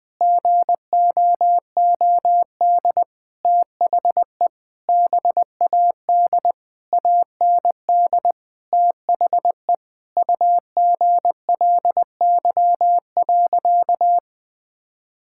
Morse code, heard as a keyed tone, about 20 wpm, 700 Hz. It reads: GOOD THE BAD AND THE UGLY.
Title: Sentences with 6-Word Phrases - 20wpm